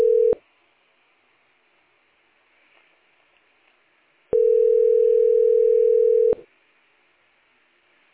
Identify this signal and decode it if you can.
Ringback tone (North America), 440 + 480 Hz.